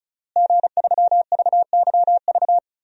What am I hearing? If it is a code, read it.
Morse code, heard as a keyed tone, about 34 wpm, 700 Hz. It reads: G3VYV
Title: DXCC-100 Call Signs - 5 Characters - Repeated 3x - 35wpm